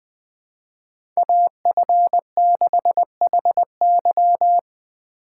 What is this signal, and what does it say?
Morse code, heard as a keyed tone, about 21 wpm, 700 Hz. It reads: AF6HY
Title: USA Call Signs - 5 Characters - 2x Character Spacing - 20wpm